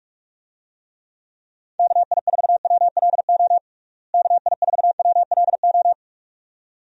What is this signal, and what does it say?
Morse code, heard as a keyed tone, about 46 wpm, 700 Hz. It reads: KI4WLO KI4WLO
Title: USA Call Signs - 6 Characters - Repeated 2x - 45wpm